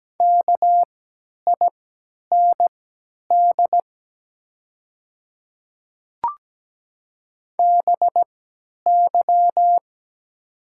Morse code, heard as a keyed tone, about 17 wpm, spaced as for 10 wpm, 700 Hz. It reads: KIND BY
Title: 3x Character Spacing - Top 300 Words - 17wpm